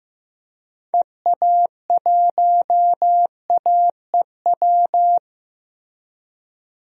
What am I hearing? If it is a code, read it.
Morse code, heard as a keyed tone, about 15 wpm, 700 Hz. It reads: EA1AEW
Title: DXCC-100 Call Signs - 6 Characters - Repeated 3x - 15wpm